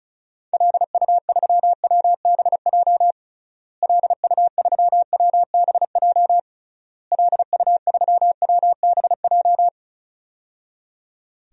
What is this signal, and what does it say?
Morse code, heard as a keyed tone, about 36 wpm, 700 Hz. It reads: LU3WBJ LU3WBJ LU3WBJ
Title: DXCC-100 Call Signs - 6 Characters - Repeated 3x - 35wpm